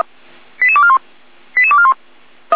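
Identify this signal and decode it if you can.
Five-tone selective call (ZVEI), heard as quick runs of five tones, tones of about 70 ms. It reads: 88241 88241
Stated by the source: ZVEI five-tone selective call 88217, received by SDR.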